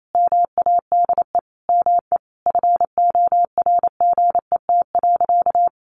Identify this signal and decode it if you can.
Morse code, heard as a keyed tone, about 28 wpm, 700 Hz. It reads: MADE ME FORGET.
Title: Sentences from Top 1500 Words 28wpm